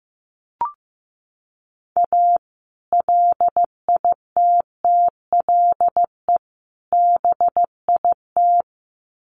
Morse code, heard as a keyed tone, about 15 wpm, 700 Hz. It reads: A LITTLE BIT
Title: Sets of 3 Words 15wpm